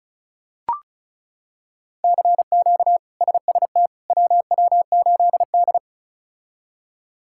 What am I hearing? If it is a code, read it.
Morse code, heard as a keyed tone, about 36 wpm, 700 Hz. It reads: CQ SST WW8D
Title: K1USN SST QSO Practice - 35wpm